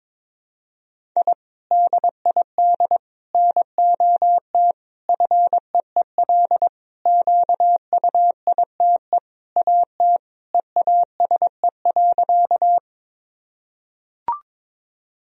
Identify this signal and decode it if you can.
Morse code, heard as a keyed tone, about 22 wpm, 700 Hz. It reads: I DID NOT FEEL QUITE AT EASE.
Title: Sentences from Top 1000 Words - Review Part 1 - 22wpm